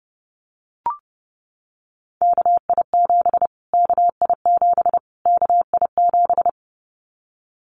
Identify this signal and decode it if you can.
Morse code, heard as a keyed tone, about 30 wpm, 700 Hz. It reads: KI7 KI7 KI7
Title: Callsign Prefixes for Top 25 DXCC - Repeated 3x - 30wpm